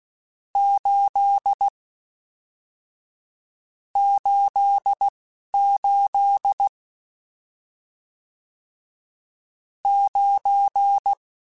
Morse code, heard as a keyed tone, about 16 wpm, 800 Hz. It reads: 8 8 8 9